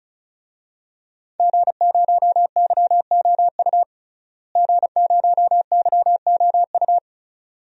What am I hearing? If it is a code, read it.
Morse code, heard as a keyed tone, about 36 wpm, 700 Hz. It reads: G0YOU G0YOU
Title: DXCC-100 Call Signs - 5 Characters - Repeated 2x - 35wpm